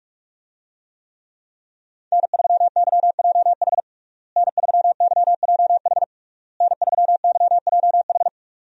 Morse code, heard as a keyed tone, about 46 wpm, 700 Hz. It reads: N3YJH N3YJH N3YJH
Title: USA Call Signs - 5 Characters - Repeated 3x - 45wpm